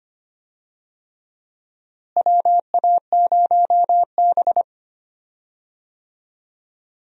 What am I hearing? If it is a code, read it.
Morse code, heard as a keyed tone, about 25 wpm, 700 Hz. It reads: WA0B
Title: USA Call Signs - 4 Characters - Repeated 3x - 25wpm